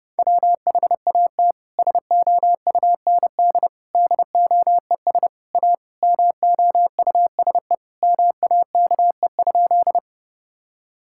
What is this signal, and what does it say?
Morse code, heard as a keyed tone, about 32 wpm, 700 Hz. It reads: WHAT SOUND DOES A MOUSE MAKE?
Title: Sentences from Top 1000 Words - Review Part 5 - 30wpm